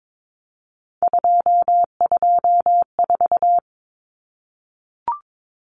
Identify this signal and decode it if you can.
Morse code, heard as a keyed tone, about 22 wpm, 700 Hz. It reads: 224
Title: Common Cut Numbers - 3-Digit 22wpm